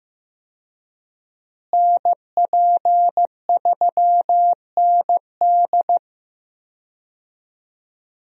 Morse code, heard as a keyed tone, about 15 wpm, 700 Hz. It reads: NP3ND
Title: USA Call Signs - 5 Characters - Repeated 3x - 15wpm